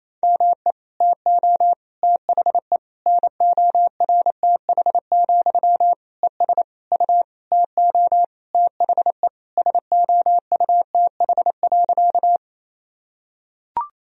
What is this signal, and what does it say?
Morse code, heard as a keyed tone, about 28 wpm, 700 Hz. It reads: ME TO THE NORTH, ES U TO THE SOUTH.
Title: Sentences from Top 400 Words and Abbreviations 28wpm